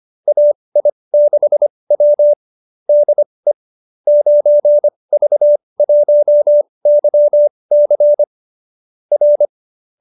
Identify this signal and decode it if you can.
Morse code, heard as a keyed tone, about 25 wpm, spaced as for 20 wpm, 570 Hz. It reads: AI6W DE 9V1YC R
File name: qso207b 20